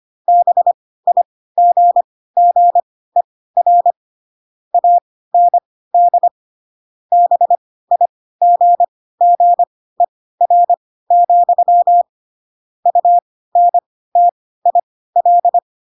Morse code, heard as a keyed tone, about 25 wpm, spaced as for 16 wpm, 700 Hz. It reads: BIGGER AND BIGGER, UNTIL